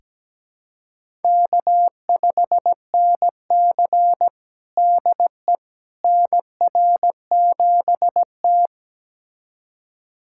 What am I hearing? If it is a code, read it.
Morse code, heard as a keyed tone, about 17 wpm, 700 Hz. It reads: K5NC DE NR7T
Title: USA Callsign DE Callsign - 4 Characters - 17wpm